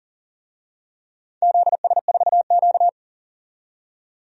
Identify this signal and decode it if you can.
Morse code, heard as a keyed tone, about 40 wpm, 690 Hz. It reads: ZS4Q
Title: DXCC-100 Call Signs - 4 Characters - 40wpm